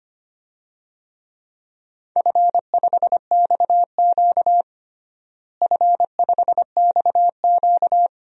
Morse code, heard as a keyed tone, about 25 wpm, 700 Hz. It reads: F5XQ F5XQ
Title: DXCC-100 Call Signs - 4 Characters - Repeated 3x - 25wpm